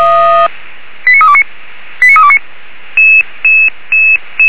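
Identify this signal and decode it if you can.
Five-tone selective call (ZVEI), heard as quick runs of five tones, tones of about 70 ms. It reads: 89329 89329
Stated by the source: ZVEI five-tone selective call received by SDR.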